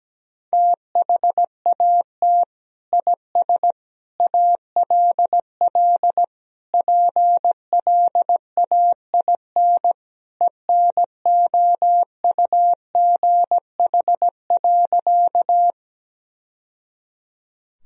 Morse code, heard as a keyed tone, about 17 wpm, 700 Hz. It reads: THAT IS ALL PLAIN ENOUGH.
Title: Sentences from Top 1000 Words - Review Part 2 - 17wpm